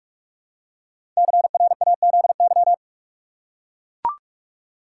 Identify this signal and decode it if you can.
Morse code, heard as a keyed tone, about 44 wpm, 700 Hz. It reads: CRAZY
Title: Five Letter Words - 45wpm